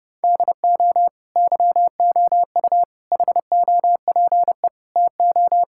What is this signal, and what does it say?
Morse code, heard as a keyed tone, about 30 wpm, 700 Hz. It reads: DO YOU HOPE TO
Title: Sentences from Top 1000 Words - Review Part 6 - 30wpm